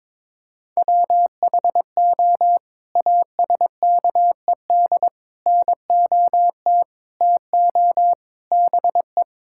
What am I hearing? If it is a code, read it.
Morse code, heard as a keyed tone, about 22 wpm, 700 Hz. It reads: WHO ASKED NOT TO BE
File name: practice / 22wpm / Sets of 7 Words - 22wpm